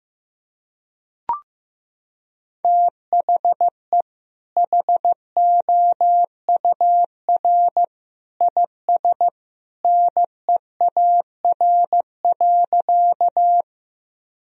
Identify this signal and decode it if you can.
Morse code, heard as a keyed tone, about 15 wpm, 700 Hz. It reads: THE HOUR IS NEAR.
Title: Sentences from Top 1000 Words - Review Part 1 - 15wpm